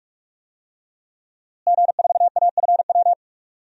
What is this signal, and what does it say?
Morse code, heard as a keyed tone, about 45 wpm, 700 Hz. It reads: G4AFW
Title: DXCC-100 Call Signs - 5 Characters - Repeated 3x - 45wpm